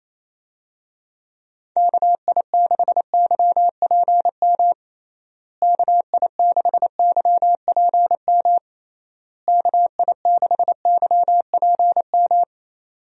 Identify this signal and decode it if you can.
Morse code, heard as a keyed tone, about 28 wpm, 700 Hz. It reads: KI6YPM KI6YPM KI6YPM
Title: USA Call Signs - 6 Characters - Repeated 3x - 28wpm